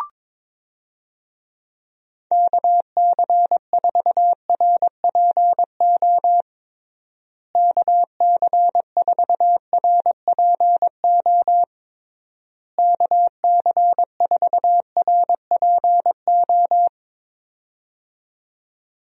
Morse code, heard as a keyed tone, about 22 wpm, 700 Hz. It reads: KC4RPO KC4RPO KC4RPO
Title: USA Call Signs - 6 Characters - Repeated 3x - 22wpm